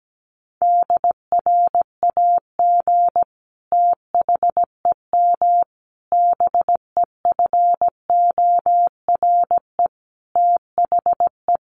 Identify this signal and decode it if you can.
Morse code, heard as a keyed tone, about 17 wpm, 700 Hz. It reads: DRAG THEM BEFORE THE
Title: Sentences from Top 1500 Words 17wpm